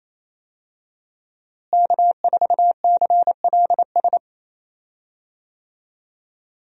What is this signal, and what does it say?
Morse code, heard as a keyed tone, about 28 wpm, 700 Hz. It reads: K4CLS
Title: USA Call Signs - 5 Characters - Repeated 2x - 28wpm